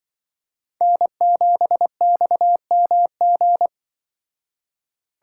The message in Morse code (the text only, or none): N7XMG